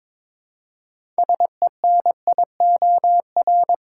Morse code, heard as a keyed tone, about 22 wpm, 700 Hz. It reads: SENIOR
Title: Six Letter Words - 22wpm